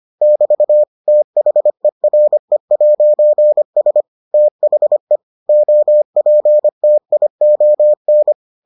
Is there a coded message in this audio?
Yes: Morse code from 0.2 to 8.3 s, heard as a keyed tone, about 25 wpm, 600 Hz. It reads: <BT> THERE'S THE OPTION